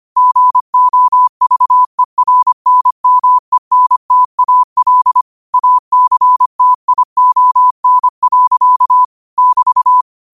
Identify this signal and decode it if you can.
Morse code, heard as a keyed tone, about 25 wpm, 1000 Hz. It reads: GOVERNMENTAL ACTION. <BT>
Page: New quotes every day in morse code at 25 Words per minute.